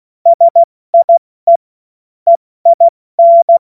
Morse code, heard as a keyed tone, about 16 wpm, 680 Hz. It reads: SIE EIN
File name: Morsekurs / Witze / Witz 13 Februar 2024 70 F80